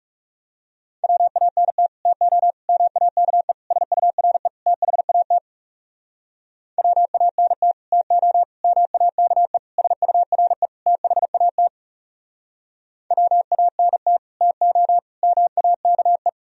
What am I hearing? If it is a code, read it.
Morse code, heard as a keyed tone, about 40 wpm, 700 Hz. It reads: WANT TO MAKE SURE THAT WANT TO MAKE SURE THAT WANT TO MAKE